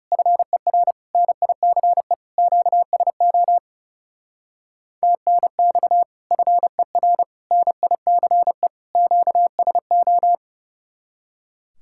Morse code, heard as a keyed tone, about 36 wpm, 700 Hz. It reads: FER NICE QSO TNX FER NICE QSO
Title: Speed-Racing Common QSO Phrases - 30wpm